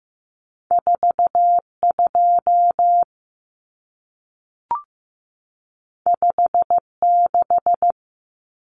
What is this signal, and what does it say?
Morse code, heard as a keyed tone, about 15 wpm, 700 Hz. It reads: 42 56
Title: Numbers - 2-Digit 15wpm